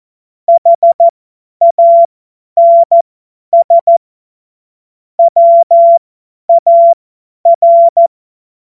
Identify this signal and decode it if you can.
Morse code, heard as a keyed tone, about 14 wpm, spaced as for 10 wpm, 680 Hz. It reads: HANS WAR
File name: Morsekurs / Witze / Witz 25. Oktober 2022 50 F70